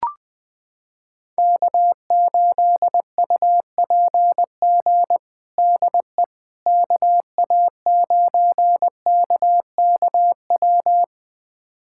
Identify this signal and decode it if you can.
Morse code, heard as a keyed tone, about 20 wpm, 700 Hz. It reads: K8UPG DE KA9KKW
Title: USA Callsign DE Callsign 20wpm